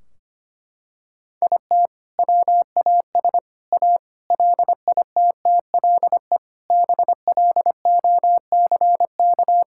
Morse code, heard as a keyed tone, about 25 wpm, 700 Hz. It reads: IT WAS A LITTLE BLOCK
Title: Sentences from Top 1600 Words 25wpm